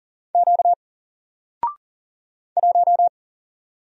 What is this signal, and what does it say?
Morse code, heard as a keyed tone, about 40 wpm, 700 Hz. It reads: Q 1
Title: Single Character (Letter:Number:Punctuation:Prosign) 40wpm